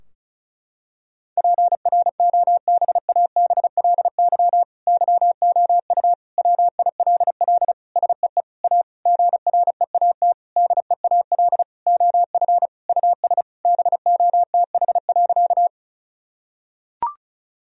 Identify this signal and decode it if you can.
Morse code, heard as a keyed tone, about 36 wpm, 700 Hz. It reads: PROBABLY YOU WILL SEE A GREAT DEAL OF US BOTH.
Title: Sentences from Top 1000 Words - Review Part 4 - 35wpm